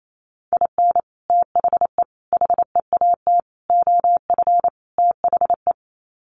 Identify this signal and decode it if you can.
Morse code, heard as a keyed tone, about 28 wpm, 700 Hz. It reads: IN THE HEAT OF THE
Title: Sets of 5 Words Encore2 28wpm